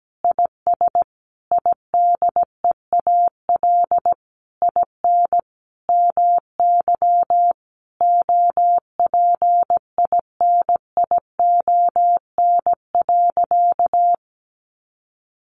Morse code, heard as a keyed tone, about 17 wpm, 700 Hz. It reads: IS IDEAL IN MY OPINION.
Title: Sentences from Top 1400 Words 17wpm